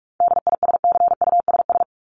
Morse code, heard as a keyed tone, about 44 wpm, 700 Hz. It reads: DISCUSS
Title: ED Suffix - Top 100 Words with Base Words 45wpm